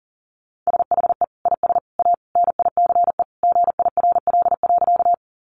Morse code, heard as a keyed tone, about 40 wpm, 700 Hz. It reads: SHE IS A NICE GIRL.
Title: Sentences from Top 1100 Words 40wpm